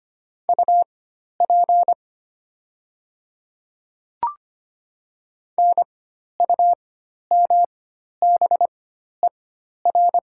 Morse code, heard as a keyed tone, about 25 wpm, spaced as for 12 wpm, 700 Hz. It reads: UP NUMBER
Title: 4x Character Spacing - Top 100 Words - 25wpm